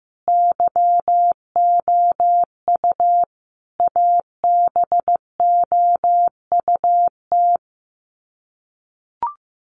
Morse code, heard as a keyed tone, about 15 wpm, 700 Hz. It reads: YOU ABOUT
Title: Sets of 5 Words Encore 15wpm